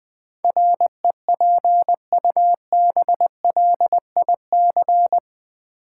Morse code, heard as a keyed tone, about 20 wpm, 700 Hz. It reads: REPUBLIC
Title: Sentences from Top 1500 Words - New Word Drill - 20wpm